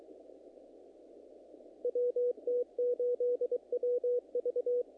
Morse code, heard as a keyed tone, about 23 wpm, 490 Hz. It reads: WT8WV
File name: contests / naqp-cw-k5tr-op-k5pi- -2000